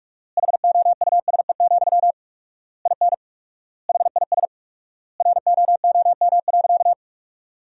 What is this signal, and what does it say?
Morse code, heard as a keyed tone, about 46 wpm, 700 Hz. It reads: HOUSE, IN HIS ROOM.
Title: 4x Word-Spacing - Sentences from Top 300 Words - 45wpm